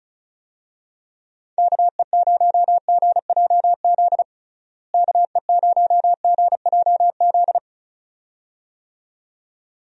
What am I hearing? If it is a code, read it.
Morse code, heard as a keyed tone, about 34 wpm, 700 Hz. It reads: KE0GJZ KE0GJZ
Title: USA Call Signs - 6 Characters - Repeated 2x - 35wpm